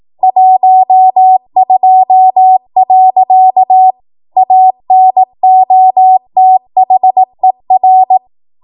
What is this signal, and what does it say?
Morse code, heard as a keyed tone, about 18 wpm, 750 Hz. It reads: 12. ANOTHER
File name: Morse / 18WPM